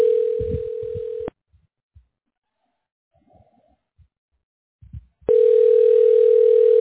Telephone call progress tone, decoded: ringback tone